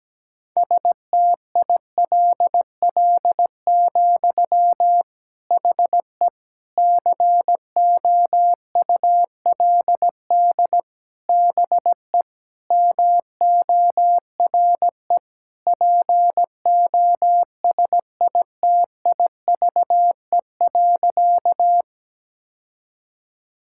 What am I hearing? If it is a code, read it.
Morse code, heard as a keyed tone, about 17 wpm, 700 Hz. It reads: STILL, HE COULD BE MORE POSITIVE.
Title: Sentences from Top 1000 Words - Review Part 5 - 17wpm